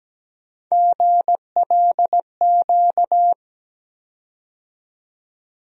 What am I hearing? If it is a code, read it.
Morse code, heard as a keyed tone, about 17 wpm, 700 Hz. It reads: GLQ
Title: USA Call Sign Postfixes - 3 Letters 17wpm